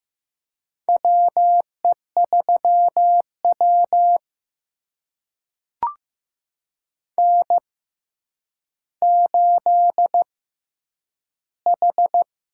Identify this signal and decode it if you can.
Morse code, heard as a keyed tone, about 15 wpm, spaced as for 9 wpm, 700 Hz. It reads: WE3W N8H
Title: USA Call Signs - 6x Character Spacing - 15wpm